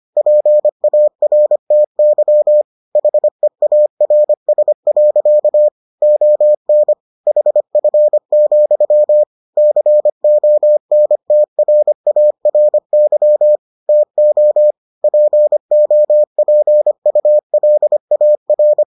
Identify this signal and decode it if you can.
Morse code, heard as a keyed tone, about 25 wpm, 600 Hz. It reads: PARTY HEARS. ON HF, CONTRARY TO POPULAR